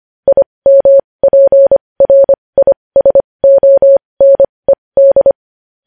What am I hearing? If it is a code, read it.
Morse code, heard as a keyed tone, about 25 wpm, spaced as for 20 wpm, 550 Hz. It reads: IMPRISONED
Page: news-headlines-in-morse-code-at-20-wpm